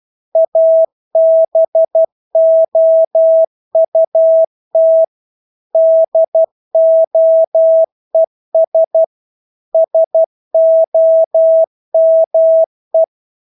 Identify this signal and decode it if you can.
Morse code, heard as a keyed tone, about 12 wpm, 650 Hz. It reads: ABOUT DOES SOME
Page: Engelske ord 12 wpm